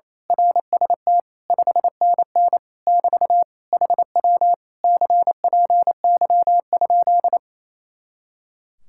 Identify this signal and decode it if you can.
Morse code, heard as a keyed tone, about 26 wpm, 700 Hz. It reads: RST 5NN <BT> HW CPY?
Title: Speed-Racing Common QSO Phrases - 28wpm